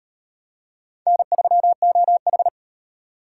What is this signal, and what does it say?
Morse code, heard as a keyed tone, about 38 wpm, 700 Hz. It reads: N3OH